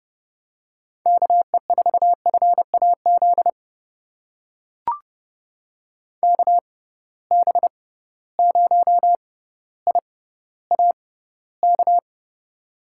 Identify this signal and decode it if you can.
Morse code, heard as a keyed tone, about 30 wpm, spaced as for 15 wpm, 700 Hz. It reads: KE4FAZ KB0IAK
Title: USA Call Signs - 6 Characters - 6x Character Spacing - 30wpm